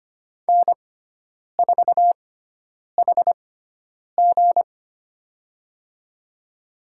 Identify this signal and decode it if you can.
Morse code, heard as a keyed tone, about 25 wpm, spaced as for 9 wpm, 700 Hz. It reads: N4HG
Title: USA Call Signs - 4 Characters - 6x Character Spacing - 25wpm